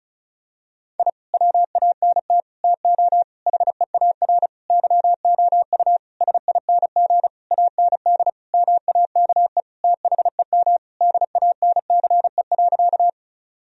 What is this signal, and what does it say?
Morse code, heard as a keyed tone, about 36 wpm, 700 Hz. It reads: I WANT TO HEAR YOU SING AND MAKE THEM DANCE.
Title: Sentences from Top 1000 Words - Review Part 4 - 35wpm